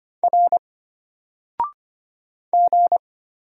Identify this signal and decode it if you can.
Morse code, heard as a keyed tone, about 25 wpm, 700 Hz. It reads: R G
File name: Single-Letter-Number-25wpm